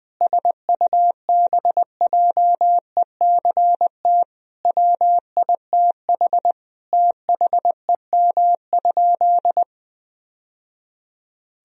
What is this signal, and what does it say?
Morse code, heard as a keyed tone, about 20 wpm, 700 Hz. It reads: SUBJECT WITH THEM?
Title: Sentences from Top 1000 Words - Review Part 1 - 20wpm